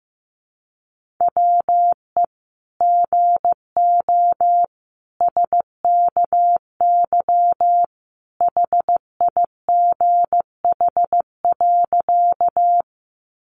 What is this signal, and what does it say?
Morse code, heard as a keyed tone, about 15 wpm, 700 Hz. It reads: WE GO SKY HIGH.
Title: Sentences from Top 1200 Words 15wpm